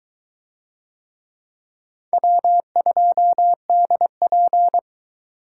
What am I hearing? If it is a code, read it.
Morse code, heard as a keyed tone, about 20 wpm, 700 Hz. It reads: W2DP